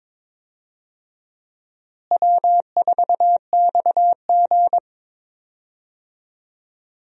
Morse code, heard as a keyed tone, about 22 wpm, 700 Hz. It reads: W4XG
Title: USA Call Signs - 4 Characters - Repeated 2x - 22wpm